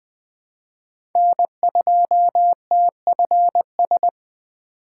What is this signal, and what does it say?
Morse code, heard as a keyed tone, about 20 wpm, 700 Hz. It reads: N2TFS
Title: USA Call Signs - 5 Characters - 2x Character Spacing - 20wpm